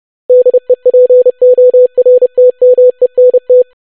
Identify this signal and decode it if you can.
Morse code, heard as a keyed tone, about 30 wpm, 500 Hz. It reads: DEPORTMENT
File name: longwords-25wpm